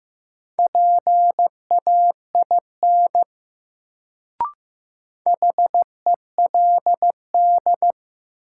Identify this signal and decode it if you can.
Morse code, heard as a keyed tone, about 15 wpm, 700 Hz. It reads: PAIN HELD
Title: Sentences from Top 1600 Words - New Word Drill - 15wpm